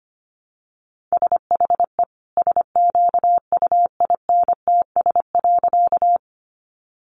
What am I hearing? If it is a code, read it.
Morse code, heard as a keyed tone, about 25 wpm, 700 Hz. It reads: SHE SQUINTS.
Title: Additional Easy Phrases 25wpm